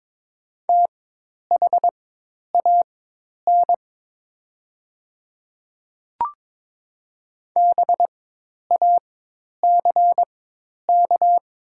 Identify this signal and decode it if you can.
Morse code, heard as a keyed tone, about 23 wpm, spaced as for 10 wpm, 700 Hz. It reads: THAN BACK
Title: 4x Character Spacing - Top 200 Words - 22wpm